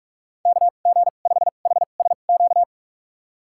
Morse code, heard as a keyed tone, about 41 wpm, spaced as for 31 wpm, 700 Hz. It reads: KG5HSQ
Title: USA Call Signs - 6 Characters - 2x Character Spacing - 45wpm